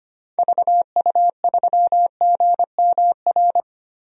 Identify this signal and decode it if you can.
Morse code, heard as a keyed tone, about 25 wpm, 700 Hz. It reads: VU3GMR
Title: DXCC-100 Call Signs - 6 Characters - Repeated 2x - 25wpm